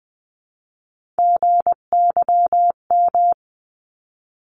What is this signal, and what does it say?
Morse code, heard as a keyed tone, about 19 wpm, 700 Hz. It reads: GYM
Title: 259 Lesson 42d - Words 18-20wpm